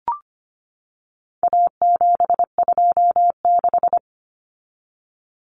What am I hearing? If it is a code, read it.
Morse code, heard as a keyed tone, about 25 wpm, 700 Hz. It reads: A726
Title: Common Cut Numbers - 4-Digit 25wpm